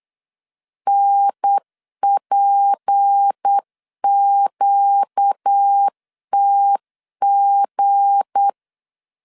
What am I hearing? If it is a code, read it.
Morse code, heard as a keyed tone, about 8 wpm, 800 Hz. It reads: NPQTG
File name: LEZIONI / 7 GF9Z / 7 G 8wpm 3118